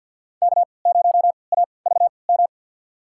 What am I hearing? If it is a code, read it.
Morse code, heard as a keyed tone, about 52 wpm, spaced as for 29 wpm, 700 Hz. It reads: K0AVM
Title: USA Call Signs - 5 Characters - 3x Character Spacing - 50wpm